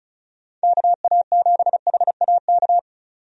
Morse code, heard as a keyed tone, about 34 wpm, 700 Hz. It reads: KA7HAK
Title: USA Call Signs - 6 Characters - 35wpm